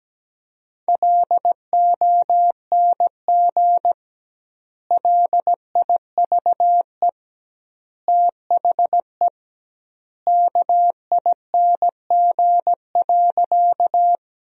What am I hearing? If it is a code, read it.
Morse code, heard as a keyed tone, about 17 wpm, 700 Hz. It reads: LONG LIVE THE KING.
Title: 2x Word-Spacing - Sentences from Top 400 Words - 17wpm